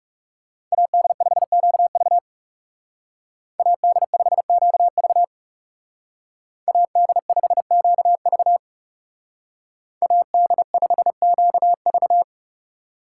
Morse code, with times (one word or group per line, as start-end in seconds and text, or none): 0.7-2.2 AD5QV
3.6-5.2 AD5QV
6.7-8.6 AD5QV
10.0-12.2 AD5QV